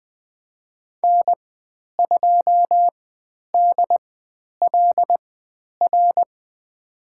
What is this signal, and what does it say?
Morse code, heard as a keyed tone, about 20 wpm, spaced as for 10 wpm, 700 Hz. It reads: N2DLR
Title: 102 Lesson 17g - Letter u - Call Signs 20wpm